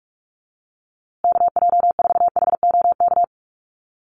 Morse code, heard as a keyed tone, about 44 wpm, 700 Hz. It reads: KJ4HOK
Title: USA Call Signs Repeated 2x 45wpm